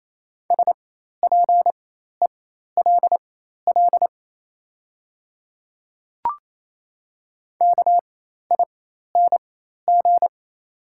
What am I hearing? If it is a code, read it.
Morse code, heard as a keyed tone, about 28 wpm, spaced as for 13 wpm, 700 Hz. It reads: SPELL KING
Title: 4x Character Spacing - Top 500 Words - 28wpm